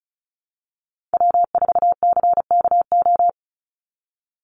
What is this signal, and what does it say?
Morse code, heard as a keyed tone, about 34 wpm, 700 Hz. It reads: W4CKO
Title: USA Call Signs 35wpm